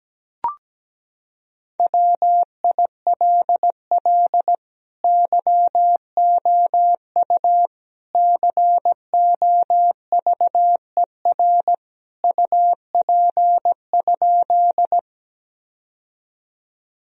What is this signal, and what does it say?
Morse code, heard as a keyed tone, about 17 wpm, 700 Hz. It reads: WILL YOU COVER UP?
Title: Sentences from Top 1000 Words - Review Part 1 - 17wpm